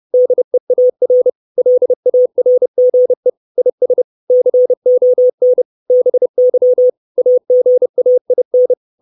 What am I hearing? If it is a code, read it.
Morse code, heard as a keyed tone, about 30 wpm, 500 Hz. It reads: DEAR LARGE IS CON BY AGAIN
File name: 632 words / 632 Words 30WPM